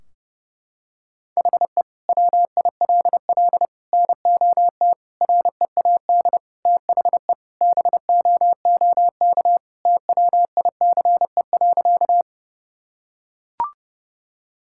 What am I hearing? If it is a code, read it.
Morse code, heard as a keyed tone, about 31 wpm, 700 Hz. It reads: HE WILL NOT READ THE BOOK TWICE.
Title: Sentences from Top 1600 Words 30wpm